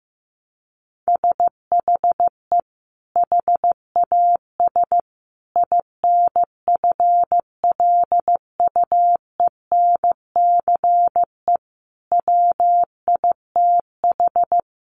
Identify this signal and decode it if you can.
Morse code, heard as a keyed tone, about 15 wpm, 700 Hz. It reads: SHE HAS INFLUENCE WITH
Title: Sentences from Top 1100 Words 15wpm